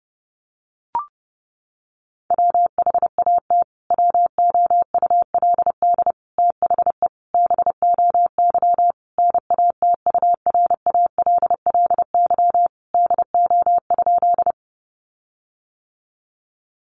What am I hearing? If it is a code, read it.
Morse code, heard as a keyed tone, about 28 wpm, 700 Hz. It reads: WHAT WOULD THE BOY NATURALLY DO?
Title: Sentences from Top 1300 Words 30wpm